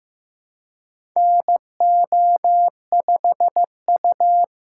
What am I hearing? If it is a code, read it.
Morse code, heard as a keyed tone, about 15 wpm, 700 Hz. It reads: NO5U
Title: USA Call Signs - 4 Characters - 2x Character Spacing - 15wpm